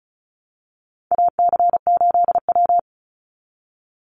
Morse code, heard as a keyed tone, about 34 wpm, 700 Hz. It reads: AC8W
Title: Call Signs Repeated 2x - 35wpm